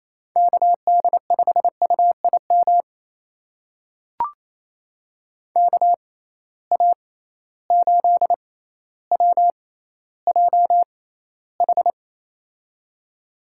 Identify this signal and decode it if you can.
Morse code, heard as a keyed tone, about 28 wpm, spaced as for 14 wpm, 700 Hz. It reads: KD5UIM KA8WJH
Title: USA Call Signs - 6 Characters - 6x Character Spacing - 28wpm